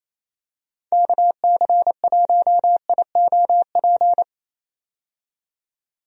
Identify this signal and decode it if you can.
Morse code, heard as a keyed tone, about 28 wpm, 700 Hz. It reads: KC1IOP
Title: USA Call Signs - 6 Characters - 28wpm